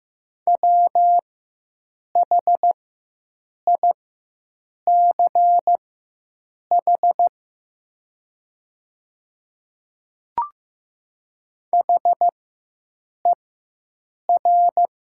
Morse code, heard as a keyed tone, about 15 wpm, spaced as for 7 wpm, 690 Hz. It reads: WHICH HER
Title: 4x Character Spacing - Top 200 Words - 15wpm